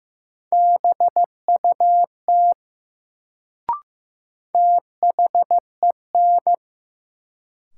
Morse code, heard as a keyed tone, about 15 wpm, 700 Hz. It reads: BUT THEN
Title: Sentences from Top 100 Words - New Word Drill - 15wpm